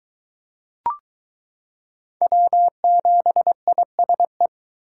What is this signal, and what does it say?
Morse code, heard as a keyed tone, about 22 wpm, 700 Hz. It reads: W7ISE